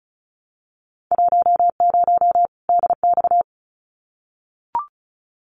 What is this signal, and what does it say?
Morse code, heard as a keyed tone, about 34 wpm, 700 Hz. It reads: 10 DX
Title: Winter Field Day Exchanges 35wpm